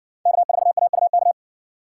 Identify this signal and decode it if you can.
Morse code, heard as a keyed tone, about 56 wpm, 700 Hz. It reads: K4AUK